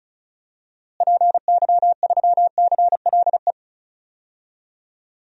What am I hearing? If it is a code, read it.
Morse code, heard as a keyed tone, about 36 wpm, 700 Hz. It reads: PY3CLE
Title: DXCC-100 Call Signs - 6 Characters - Repeated 2x - 35wpm